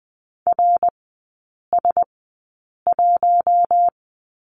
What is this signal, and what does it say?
Morse code, heard as a keyed tone, about 20 wpm, spaced as for 8 wpm, 690 Hz. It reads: RS1
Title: 060 Lesson 11e - Letter h - Three Characters 20wpm